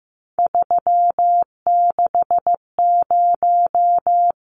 Morse code, heard as a keyed tone, about 15 wpm, 700 Hz. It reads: 360
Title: Numbers - 3-Digit 15wpm